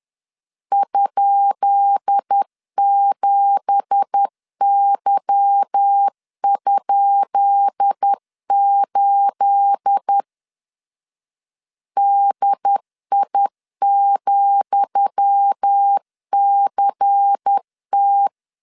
Morse code, heard as a keyed tone, about 10 wpm, 800 Hz. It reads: ?7Y?8 DI,CT
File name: LEZIONI / 10 SIGNS / 10D COMA 10wpm 3013